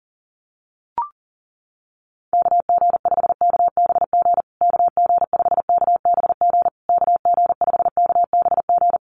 Morse code, heard as a keyed tone, about 40 wpm, 700 Hz. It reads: KG5KBG KG5KBG KG5KBG
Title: USA Call Signs Repeated 3x 40wpm